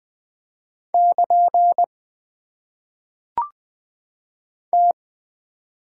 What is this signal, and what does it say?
Morse code, heard as a keyed tone, about 20 wpm, 700 Hz. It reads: <KN> T
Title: Top 90 QSO Elements - 6x Character Spacing - 20wpm